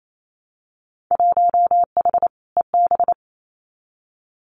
Morse code, heard as a keyed tone, about 28 wpm, 700 Hz. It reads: 1H EB
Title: Winter Field Day Exchanges 28wpm